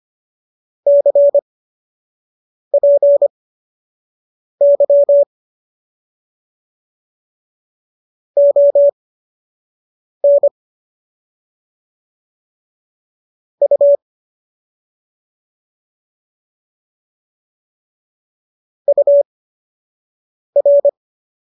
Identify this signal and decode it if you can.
Morse code, heard as a keyed tone, about 25 wpm, spaced as for 6 wpm, 570 Hz. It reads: CPY ON U UR